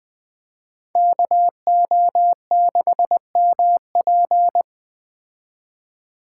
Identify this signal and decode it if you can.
Morse code, heard as a keyed tone, about 21 wpm, 700 Hz. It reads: KO6MP
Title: Speed Racing - USA 2x2 Call Signs 17wpm